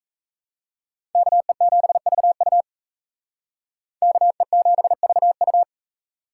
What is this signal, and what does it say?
Morse code, heard as a keyed tone, about 35 wpm, 700 Hz. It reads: KE7VU KE7VU
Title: Speed Racing - USA 2x2 Call Signs 28wpm